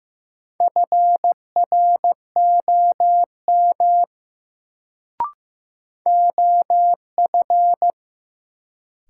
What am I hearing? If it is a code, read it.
Morse code, heard as a keyed tone, about 15 wpm, 700 Hz. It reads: FROM OF
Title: Sentences from Top 100 Words - New Word Drill - 15wpm